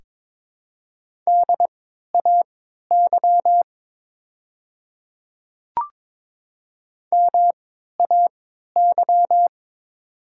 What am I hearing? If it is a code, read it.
Morse code, heard as a keyed tone, about 22 wpm, spaced as for 12 wpm, 700 Hz. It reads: DAY MAY
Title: 3x Character Spacing - Top 100 Words - 22wpm